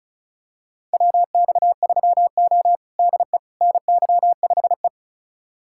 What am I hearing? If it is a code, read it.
Morse code, heard as a keyed tone, about 36 wpm, 700 Hz. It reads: WX3O DE NY5E
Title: USA Callsign DE Callsign - 4 Characters - 35wpm